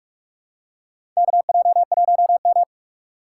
Morse code, heard as a keyed tone, about 45 wpm, 690 Hz. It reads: KJ1M
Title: USA Call Signs - 4 Characters - 3x Character Spacing - 45wpm